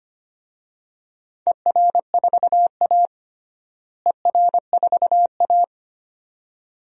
Morse code, heard as a keyed tone, about 25 wpm, 700 Hz. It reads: ER4A ER4A
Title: DXCC-100 Call Signs - 4 Characters - Repeated 2x - 25wpm